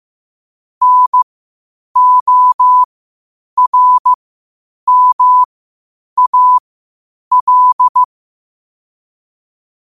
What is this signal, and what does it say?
Morse code, heard as a keyed tone, about 15 wpm, spaced as for 8 wpm, 1000 Hz. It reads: NORMAL
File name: qotdfiles / 8wpm